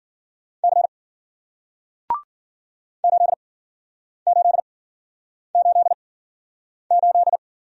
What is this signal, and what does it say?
Morse code, heard as a keyed tone, about 56 wpm, spaced as for 9 wpm, 700 Hz. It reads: C 8888